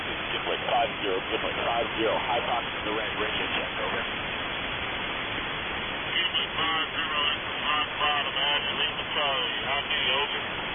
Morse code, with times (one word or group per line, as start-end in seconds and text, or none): none